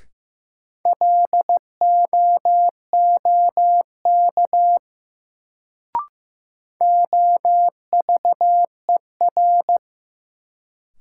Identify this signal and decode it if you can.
Morse code, heard as a keyed tone, about 15 wpm, 700 Hz. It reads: LOOK OVER
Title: Sets of 2 Words - New Word Drill - 15wpm